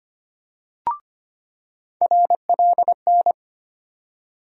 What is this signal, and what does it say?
Morse code, heard as a keyed tone, about 25 wpm, 700 Hz. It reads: RLN
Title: USA Call Sign Postfixes - 3 Letters 25wpm